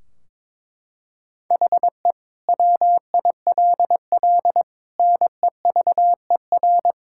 Morse code, heard as a keyed tone, about 22 wpm, 700 Hz. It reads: HE WILL NEVER
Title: Sentences from Top 1600 Words 22wpm